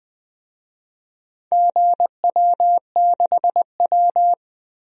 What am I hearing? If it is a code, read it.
Morse code, heard as a keyed tone, about 21 wpm, 700 Hz. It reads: GW6W
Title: DXCC-100 Call Signs - 4 Characters - Repeated 2x - 20wpm